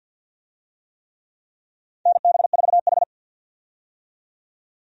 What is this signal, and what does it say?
Morse code, heard as a keyed tone, about 47 wpm, 700 Hz. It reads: NB4H
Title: USA Call Signs - 4 Characters - 50wpm